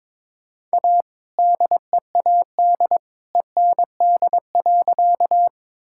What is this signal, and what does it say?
Morse code, heard as a keyed tone, about 22 wpm, 700 Hz. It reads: A DEAD END.
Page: morse-code-ninja-practice-22wpm